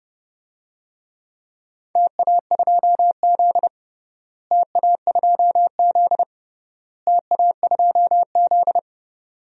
Morse code, heard as a keyed tone, about 30 wpm, 700 Hz. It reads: TA2Z TA2Z TA2Z
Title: DXCC-100 Call Signs - 4 Characters - Repeated 3x - 30wpm